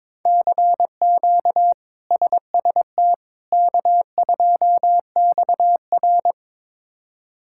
Morse code, heard as a keyed tone, about 21 wpm, 700 Hz. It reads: CQ SST K2XR
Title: K1USN SST QSO Practice - 22wpm